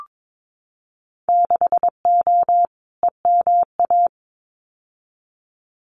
Morse code, heard as a keyed tone, about 22 wpm, 700 Hz. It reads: 6O EMA
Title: Winter Field Day Exchanges Encore 22wpm